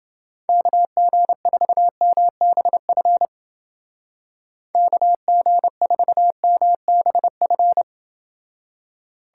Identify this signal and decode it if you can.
Morse code, heard as a keyed tone, about 29 wpm, 700 Hz. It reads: KG4MBF KG4MBF